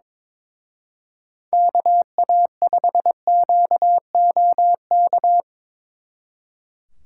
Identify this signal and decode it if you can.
Morse code, heard as a keyed tone, about 26 wpm, 700 Hz. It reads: KA5QOK
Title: Speed Racing - USA 2x3 Call Signs 22wpm